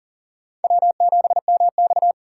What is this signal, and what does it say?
Morse code, heard as a keyed tone, about 37 wpm, 700 Hz. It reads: W7MX
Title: USA Call Signs - 4 Characters - 3x Character Spacing - 40wpm